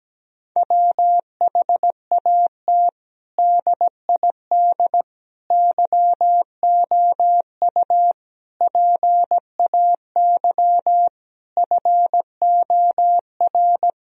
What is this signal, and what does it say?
Morse code, heard as a keyed tone, about 17 wpm, 700 Hz. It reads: WHAT DID YOU PAY FOR
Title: Sentences from Top 1000 Words - Review Part 3 - 17wpm